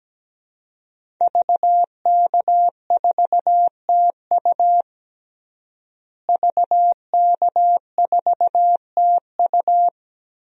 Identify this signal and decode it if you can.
Morse code, heard as a keyed tone, about 17 wpm, 700 Hz. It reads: VK4TU VK4TU
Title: DXCC-100 Call Signs - 5 Characters - Repeated 3x - 17wpm